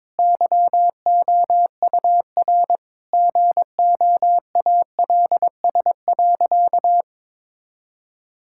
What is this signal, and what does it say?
Morse code, heard as a keyed tone, about 22 wpm, 700 Hz. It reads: YOUR GOALS.
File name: Difficult Sentences with Hints - 22wpm